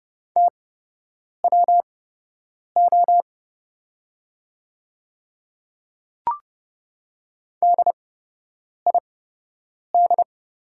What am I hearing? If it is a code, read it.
Morse code, heard as a keyed tone, about 31 wpm, spaced as for 8 wpm, 700 Hz. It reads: TWO DID
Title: 8x Character Spacing - Top 100 Words - 30wpm